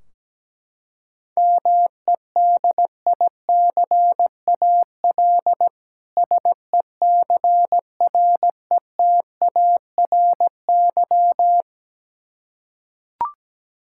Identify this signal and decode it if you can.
Morse code, heard as a keyed tone, about 17 wpm, 700 Hz. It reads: MEDICAL SECRETARY
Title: K1USN SST QSO Practice 17wpm